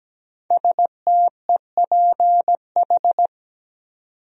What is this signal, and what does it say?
Morse code, heard as a keyed tone, about 17 wpm, 700 Hz. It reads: STEPH
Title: USA Common Names - 17wpm